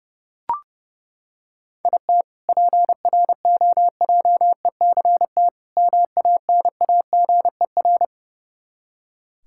Morse code, heard as a keyed tone, about 30 wpm, 700 Hz. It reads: IT PROJECT MANAGER
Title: K1USN SST QSO Practice 30wpm